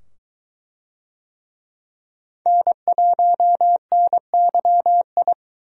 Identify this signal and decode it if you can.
Morse code, heard as a keyed tone, about 23 wpm, 700 Hz. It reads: N1NYI